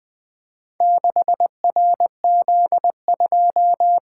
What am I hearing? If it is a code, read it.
Morse code, heard as a keyed tone, about 20 wpm, 700 Hz. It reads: 6RZ2
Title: Four Letter-Number Combinations - 20wpm